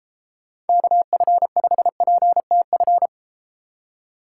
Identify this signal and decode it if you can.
Morse code, heard as a keyed tone, about 33 wpm, 700 Hz. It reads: KF5PTF